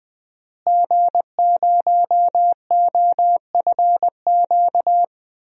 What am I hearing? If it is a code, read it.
Morse code, heard as a keyed tone, about 20 wpm, 700 Hz. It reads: G0OFQ
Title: DXCC-100 Call Signs - 5 Characters - Repeated 2x - 20wpm